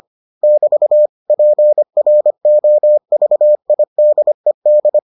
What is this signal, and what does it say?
Morse code, heard as a keyed tone, about 25 wpm, 600 Hz. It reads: <BT> PROVIDED